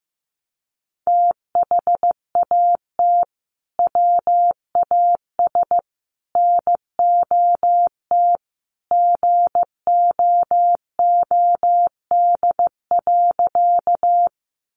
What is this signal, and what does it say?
Morse code, heard as a keyed tone, about 15 wpm, 700 Hz. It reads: THAT WAS NOT GOOD.
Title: Sentences from Top 600 Words 15wpm